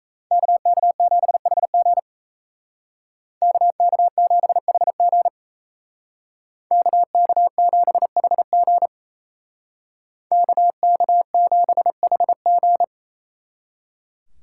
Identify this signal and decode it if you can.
Morse code, heard as a keyed tone, about 33 wpm, 700 Hz. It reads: KK7HG KK7HG KK7HG KK7HG
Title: Speed Racing - USA 2x2 Call Signs 28wpm